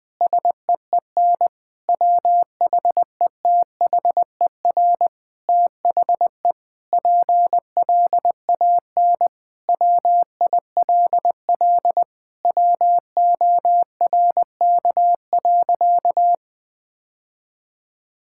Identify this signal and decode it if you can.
Morse code, heard as a keyed tone, about 20 wpm, 700 Hz. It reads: SEEN WHETHER THE PLAN WILL WORK.
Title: Sentences with 7-Word Phrases - 20wpm